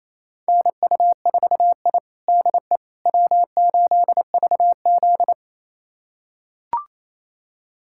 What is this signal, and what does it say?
Morse code, heard as a keyed tone, about 28 wpm, 700 Hz. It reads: NU4I DE W8VZ
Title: USA Callsign DE Callsign - 4 Characters - 28wpm